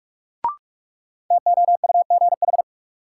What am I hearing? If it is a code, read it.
Morse code, heard as a keyed tone, about 46 wpm, 700 Hz. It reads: TOUGH
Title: Sentences from Top 1200 Words - New Word Drill Alt - 45wpm